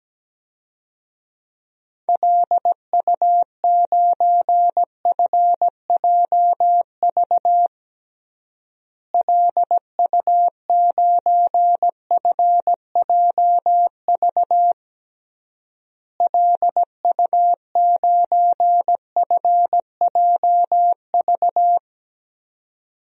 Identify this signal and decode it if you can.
Morse code, heard as a keyed tone, about 17 wpm, 700 Hz. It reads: LU9FJV LU9FJV LU9FJV
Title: DXCC-100 Call Signs - 6 Characters - Repeated 3x - 17wpm